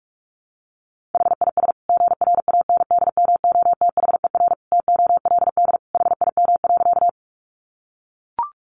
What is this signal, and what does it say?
Morse code, heard as a keyed tone, about 44 wpm, 700 Hz. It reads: HIS GRANDMOTHER TOLD HIM.
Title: Sentences from Top 1200 Words 45wpm